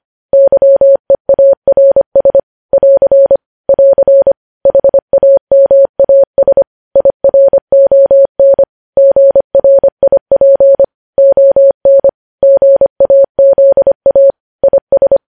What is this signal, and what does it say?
Morse code, heard as a keyed tone, about 25 wpm, 550 Hz. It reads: YEARS <AR> <AR> HAMAS IRON GRIP ON GAZA IS
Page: News Headlines in Morse Code at 25 WPM